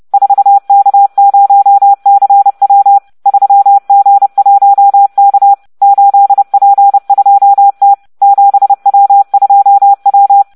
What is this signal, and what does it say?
Morse code, heard as a keyed tone, about 30 wpm, 800 Hz. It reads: 4K0CW 3G1K 8P2T 7W2W